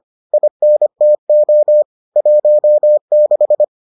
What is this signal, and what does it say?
Morse code, heard as a keyed tone, about 25 wpm, 600 Hz. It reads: INTO 16